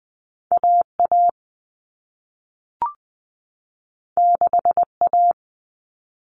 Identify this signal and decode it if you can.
Morse code, heard as a keyed tone, about 20 wpm, 700 Hz. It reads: AA 6A
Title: All Cut Numbers - 2-Digit 20wpm